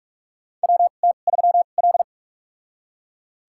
Morse code, heard as a keyed tone, about 41 wpm, spaced as for 31 wpm, 700 Hz. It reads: WT3L
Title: USA Call Signs - 2x Character Spacing - 45wpm